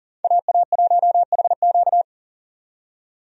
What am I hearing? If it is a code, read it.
Morse code, heard as a keyed tone, about 40 wpm, 700 Hz. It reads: AA1HQ
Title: USA Call Signs - 5 Characters - 40wpm